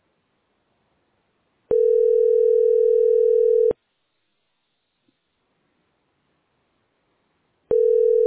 Ringback tone (North America), 440 + 480 Hz.